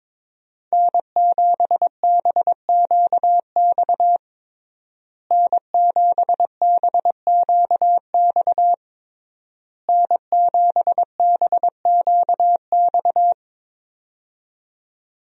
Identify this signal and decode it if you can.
Morse code, heard as a keyed tone, about 22 wpm, 700 Hz. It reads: N7BQX N7BQX N7BQX
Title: USA Call Signs - 5 Characters - Repeated 3x - 22wpm